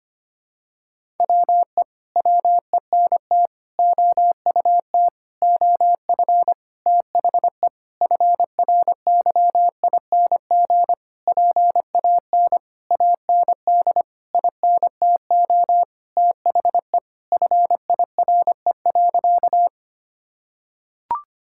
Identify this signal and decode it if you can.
Morse code, heard as a keyed tone, about 25 wpm, 700 Hz. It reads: WE WENT OUT OF THE FRYING PAN AND INTO THE FIRE.
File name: practice / 25wpm / Sentences with 9-Word Phrases - 25wpm